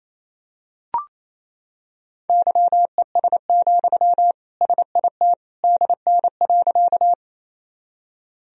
Morse code, heard as a keyed tone, about 28 wpm, 700 Hz. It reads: YES, SIT DN.
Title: Sentences from Top 500 Words and Abbreviations 28wpm